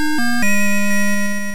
There is no code.